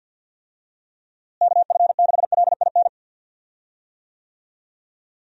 Morse code, heard as a keyed tone, about 51 wpm, 700 Hz. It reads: KF6LIN
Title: USA Call Signs - 6 Characters - Repeated 2x - 50wpm